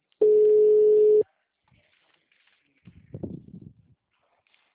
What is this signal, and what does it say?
Ringback tone (Europe, 425 Hz).
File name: out- -2002- - - .1731